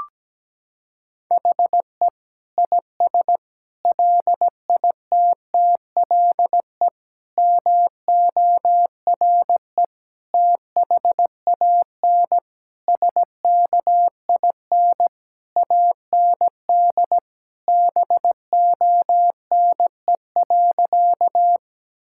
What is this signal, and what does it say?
Morse code, heard as a keyed tone, about 17 wpm, 700 Hz. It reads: HE IS LITTLE MORE THAN SKIN AND BONE.
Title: Sentences from Top 1600 Words 17wpm